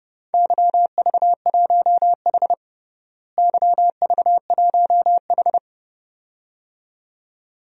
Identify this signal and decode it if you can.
Morse code, heard as a keyed tone, about 32 wpm, 700 Hz. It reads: YV1H YV1H
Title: DXCC-100 Call Signs - 4 Characters - Repeated 2x - 30wpm